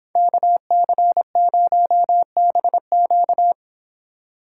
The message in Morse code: KC0BQ

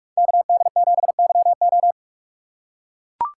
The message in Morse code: KD7YO

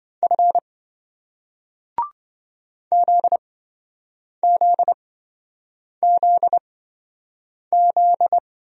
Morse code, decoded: F ZZZZ

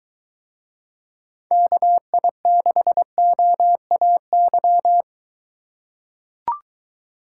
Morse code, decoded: KI6OAY

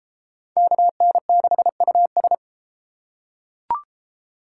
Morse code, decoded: KN6US

